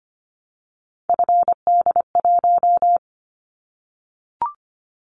Morse code, heard as a keyed tone, about 25 wpm, 700 Hz. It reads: FD1
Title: Callsign Prefixes for Top 50 DXCC - 25wpm